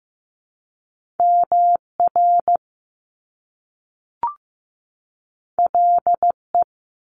Morse code, heard as a keyed tone, about 15 wpm, 700 Hz. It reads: MR LE
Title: Top Two Letter Combinations in Words 15wpm